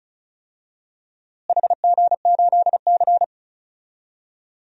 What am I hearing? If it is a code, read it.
Morse code, heard as a keyed tone, about 34 wpm, 700 Hz. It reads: HG8C
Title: DXCC-100 Call Signs - 4 Characters - Repeated 3x - 35wpm